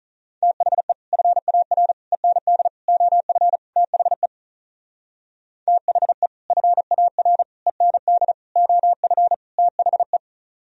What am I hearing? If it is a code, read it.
Morse code, heard as a keyed tone, about 43 wpm, 700 Hz. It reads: THE FAR END OF THE THE FAR END OF THE
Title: Speed-Racing - Sets of 5 Words 35wpm